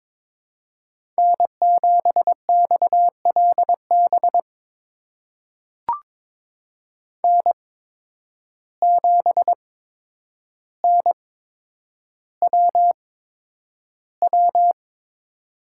Morse code, heard as a keyed tone, about 21 wpm, spaced as for 9 wpm, 700 Hz. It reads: N7XLB N7NWW
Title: USA Call Signs - 5 Characters - 8x Character Spacing - 22wpm